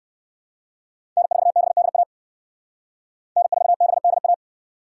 Morse code, heard as a keyed tone, about 66 wpm, 700 Hz. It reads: N4BDA N4BDA